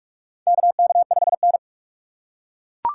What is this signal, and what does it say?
Morse code, heard as a keyed tone, about 42 wpm, 700 Hz. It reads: KK5N
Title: USA Call Signs - 3x Character Spacing - 45wpm